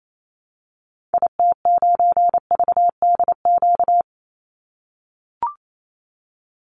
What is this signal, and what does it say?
Morse code, heard as a keyed tone, about 28 wpm, 700 Hz. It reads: IT9VDQ
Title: Call Signs - 28wpm